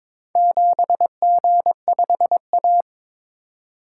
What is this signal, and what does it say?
Morse code, heard as a keyed tone, about 22 wpm, 700 Hz. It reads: 7G5A
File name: practice / 22wpm / Four Letter-Number Combinations - 22wpm